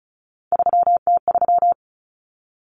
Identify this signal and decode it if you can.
Morse code, heard as a keyed tone, about 34 wpm, 700 Hz. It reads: 3T3
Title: Common Cut Numbers - 3-Digit 35wpm